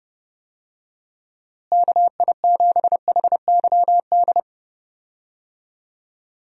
Morse code, heard as a keyed tone, about 31 wpm, 700 Hz. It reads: KI7HYD
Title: USA Call Signs - 6 Characters - Repeated 3x - 30wpm